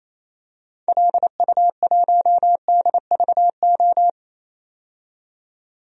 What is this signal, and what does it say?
Morse code, heard as a keyed tone, about 28 wpm, 700 Hz. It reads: LU1DVO
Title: DXCC-100 Call Signs - 6 Characters - 28wpm